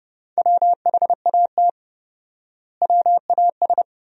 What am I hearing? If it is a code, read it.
Morse code, heard as a keyed tone, about 32 wpm, 700 Hz. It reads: WHAT WAS
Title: 4x Word-Spacing - Sentences from Top 100 Words - 30wpm